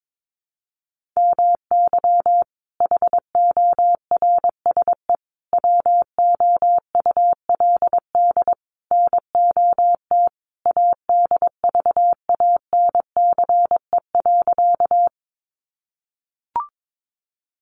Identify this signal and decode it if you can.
Morse code, heard as a keyed tone, about 22 wpm, 700 Hz. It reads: MY HORSE WOULD NOT ADVANCE.
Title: Sentences from Top 1400 Words 22wpm